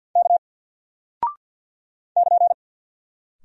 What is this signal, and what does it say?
Morse code, heard as a keyed tone, about 48 wpm, 700 Hz. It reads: K <KN>
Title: Single Character (Letter:Number:Punctuation:Prosign) Alt 50wpm